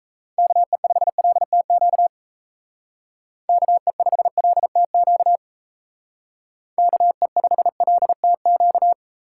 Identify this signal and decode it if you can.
Morse code, heard as a keyed tone, about 33 wpm, 700 Hz. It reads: KE5LTQ KE5LTQ KE5LTQ
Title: Speed Racing - USA 2x3 Call Signs 28wpm